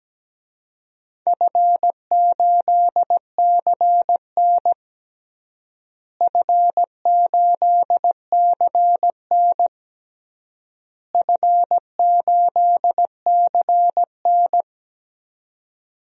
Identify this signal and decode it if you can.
Morse code, heard as a keyed tone, about 17 wpm, 700 Hz. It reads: F8CN F8CN F8CN
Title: DXCC-100 Call Signs - 4 Characters - Repeated 3x - 17wpm